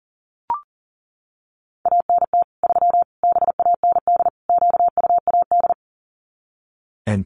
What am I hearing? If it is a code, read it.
Morse code, heard as a keyed tone, about 38 wpm, 700 Hz. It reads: ANT 3 BAND QUAD
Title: Common QSO Phrases 40wpm